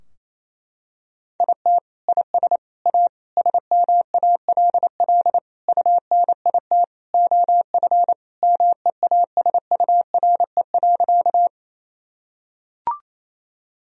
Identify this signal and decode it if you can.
Morse code, heard as a keyed tone, about 28 wpm, 700 Hz. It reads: IT IS A SMALL UNIT OF MEASURE.
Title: Sentences from Top 1000 Words - Review Part 2 - 28wpm